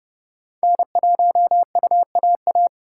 Morse code, heard as a keyed tone, about 29 wpm, 700 Hz. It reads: N1UAA